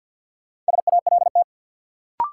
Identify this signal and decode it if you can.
Morse code, heard as a keyed tone, about 53 wpm, 700 Hz. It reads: SALT
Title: Sets of 5 Words - New Word Drill - 50wpm